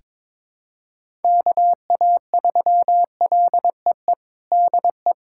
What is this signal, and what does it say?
Morse code, heard as a keyed tone, about 22 wpm, 700 Hz. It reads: KA3LEE DE
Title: USA Callsign DE Callsign - 6 Characters - 22wpm